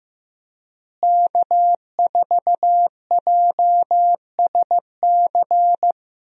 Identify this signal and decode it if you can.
Morse code, heard as a keyed tone, about 15 wpm, 700 Hz. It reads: K4JSC
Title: USA Call Signs - 5 Characters - Repeated 3x - 15wpm